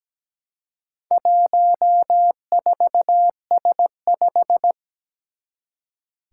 Morse code, heard as a keyed tone, about 17 wpm, 700 Hz. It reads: 14S5
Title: Four Challenging Letter-Number Combinations 17wpm